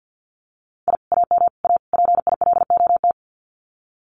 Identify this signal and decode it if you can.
Morse code, heard as a keyed tone, about 50 wpm, 700 Hz. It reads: I AM A PILOT
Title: Common QSO Phrases Encore 50wpm